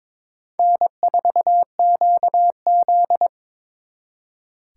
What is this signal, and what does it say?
Morse code, heard as a keyed tone, about 26 wpm, 700 Hz. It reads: N4QZ
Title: Speed Racing - USA 1x2 Call Signs 22wpm